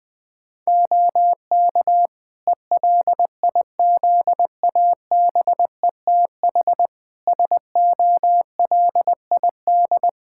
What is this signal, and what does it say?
Morse code, heard as a keyed tone, about 21 wpm, 700 Hz. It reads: OK ELIZABETH SOLID
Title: Practice QSO - Exchange 3 - More Info - 20wpm